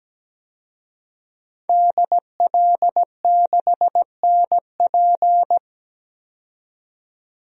Morse code, heard as a keyed tone, about 17 wpm, 700 Hz. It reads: DL6NP
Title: DXCC-100 Call Signs - 5 Characters - 17wpm